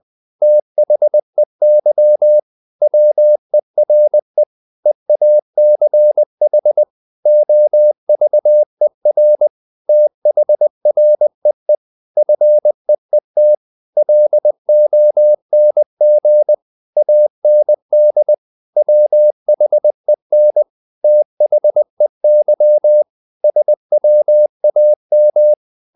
Morse code, heard as a keyed tone, about 20 wpm, 600 Hz. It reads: THEY WERE EACH OVER THREE FEET LONG AND WHEN THEY SWAM
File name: the old man and the sea part 2 20wpm